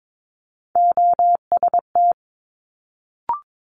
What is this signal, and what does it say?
Morse code, heard as a keyed tone, about 22 wpm, 700 Hz. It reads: OST
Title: Multi-letter Phonemes Quick Fire 22wpm